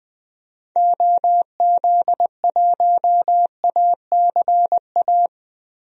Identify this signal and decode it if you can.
Morse code, heard as a keyed tone, about 21 wpm, 700 Hz. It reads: OZ1ACA
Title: DXCC-100 Call Signs - 6 Characters - Repeated 3x - 20wpm